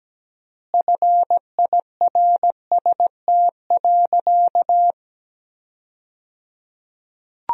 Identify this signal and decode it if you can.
Morse code, heard as a keyed tone, about 17 wpm, 700 Hz. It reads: FIRST.
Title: 3x Word-Spacing - Sentences from Top 300 Words - 17wpm